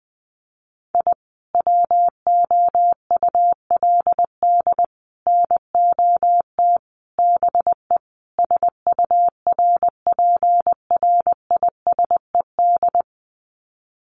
Morse code, heard as a keyed tone, about 20 wpm, 700 Hz. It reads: I WOULD NOT BE SURPRISED
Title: Sets of 5 Words Encore2 20wpm